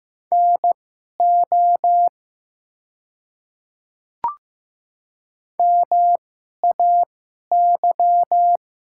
Morse code, heard as a keyed tone, about 15 wpm, spaced as for 11 wpm, 700 Hz. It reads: NO MAY
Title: 2x Character Spacing - Top 100 Words - 15wpm